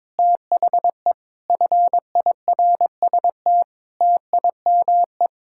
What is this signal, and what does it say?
Morse code, heard as a keyed tone, about 21 wpm, 700 Hz. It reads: THE FIRST TIME
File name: practice / 22wpm / Sentences with 4-Word Phrases - 22wpm